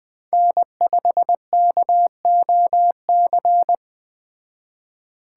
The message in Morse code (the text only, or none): N5KOC